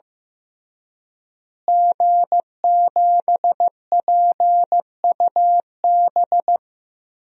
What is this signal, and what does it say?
Morse code, heard as a keyed tone, about 15 wpm, 700 Hz. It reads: G7PUB
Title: DXCC-100 Call Signs - 5 Characters - Repeated 3x - 15wpm